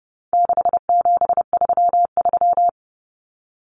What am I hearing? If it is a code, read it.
Morse code, heard as a keyed tone, about 30 wpm, 700 Hz. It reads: 6733
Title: Numbers - 4-Digit 30wpm